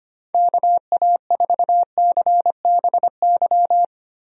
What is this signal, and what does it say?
Morse code, heard as a keyed tone, about 25 wpm, 700 Hz. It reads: KA4CBY
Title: USA Call Signs - 6 Characters - 25wpm